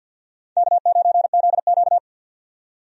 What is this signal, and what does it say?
Morse code, heard as a keyed tone, about 50 wpm, 700 Hz. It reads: K9ZQ
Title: USA Call Signs - 4 Characters - 50wpm